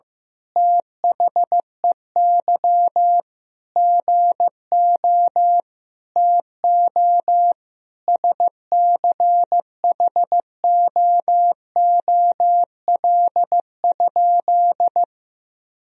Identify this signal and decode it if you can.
Morse code, heard as a keyed tone, about 15 wpm, 700 Hz. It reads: THEY GO TO SCHOOL?
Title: Sentences from Top 1000 Words - Review Part 1 - 15wpm